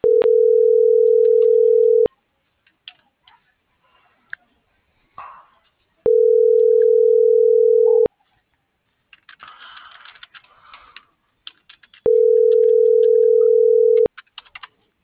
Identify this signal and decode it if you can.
Ringback tone (North America), 440 + 480 Hz.
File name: CENTRAL MABE / out maria - .71719